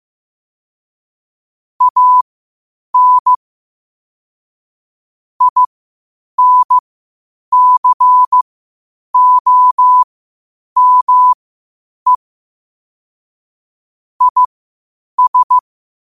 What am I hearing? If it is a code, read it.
Morse code, heard as a keyed tone, about 15 wpm, spaced as for 8 wpm, 1000 Hz. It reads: AN INCOME IS
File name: qotdfiles / 8wpm